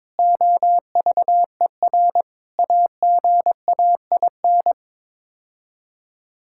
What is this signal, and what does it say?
Morse code, heard as a keyed tone, about 21 wpm, 700 Hz. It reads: OVER AGAIN
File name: practice / 22wpm / Sets of 7 Words - 22wpm